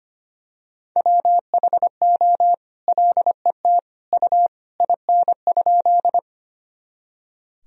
Morse code, heard as a keyed tone, about 25 wpm, 700 Hz. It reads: WHO LET U IN?
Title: Sentences from Top 300 Words and Abbreviations 25wpm